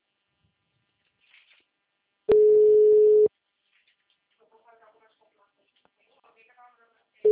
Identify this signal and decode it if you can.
Ringback tone (Europe, 425 Hz), with speech.